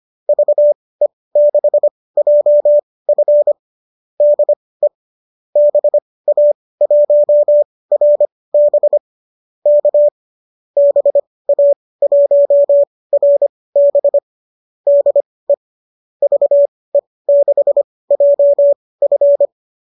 Morse code, heard as a keyed tone, about 25 wpm, spaced as for 18 wpm, 590 Hz. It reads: VE6JF DE BA1RB K BA1RB DE VE6JF
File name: qso203 18